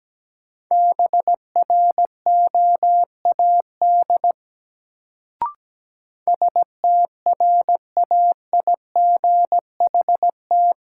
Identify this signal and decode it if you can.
Morse code, heard as a keyed tone, about 17 wpm, 700 Hz. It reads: BROAD STRAIGHT
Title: Sentences from Top 1000 Words - New Word Drill - 17wpm